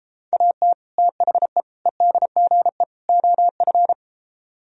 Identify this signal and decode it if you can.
Morse code, heard as a keyed tone, about 32 wpm, 700 Hz. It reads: AT THE EDGE OF